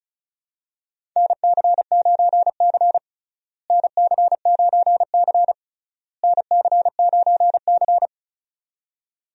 Morse code, heard as a keyed tone, about 36 wpm, 700 Hz. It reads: NC9C NC9C NC9C
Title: USA Call Signs - 4 Characters - Repeated 3x - 35wpm